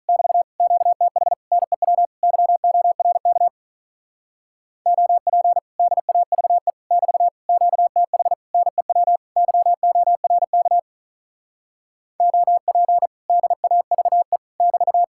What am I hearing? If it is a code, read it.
Morse code, heard as a keyed tone, about 46 wpm, 700 Hz. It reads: <BT> QTH NEW YORK OP DAVE <BT> QTH NEW YORK OP DAVE <BT>